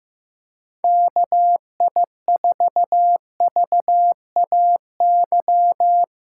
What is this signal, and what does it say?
Morse code, heard as a keyed tone, about 15 wpm, 700 Hz. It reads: KI4VAY
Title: USA Call Signs - 6 Characters - 3x Character Spacing - 15wpm